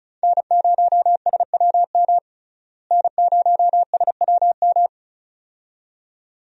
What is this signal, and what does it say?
Morse code, heard as a keyed tone, about 36 wpm, 700 Hz. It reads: N0SWM N0SWM
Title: USA Call Signs - 5 Characters - Repeated 2x - 35wpm